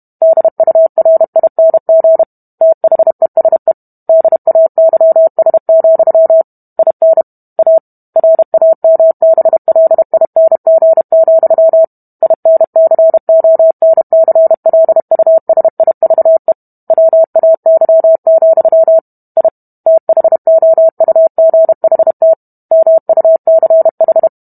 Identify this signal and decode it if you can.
Morse code, heard as a keyed tone, about 32 wpm, 650 Hz. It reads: DURING THESE DAYS, IN A RAMBLING, INCONCLUSIVE WAY, I THOUGHT MUCH